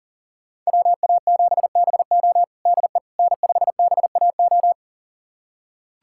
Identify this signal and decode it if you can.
Morse code, heard as a keyed tone, about 40 wpm, 700 Hz. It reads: WA7BO DE N5BAO
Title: USA Callsign DE Callsign - 5 Characters - 40wpm